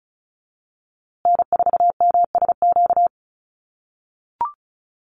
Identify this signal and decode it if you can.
Morse code, heard as a keyed tone, about 34 wpm, 700 Hz. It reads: N4MSQ
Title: USA Call Signs Repeated 2x 35wpm